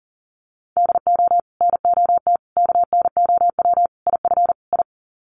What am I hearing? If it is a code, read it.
Morse code, heard as a keyed tone, about 40 wpm, 700 Hz. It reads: DO NOT KNOW IF I
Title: Sets of 5 Words 40wpm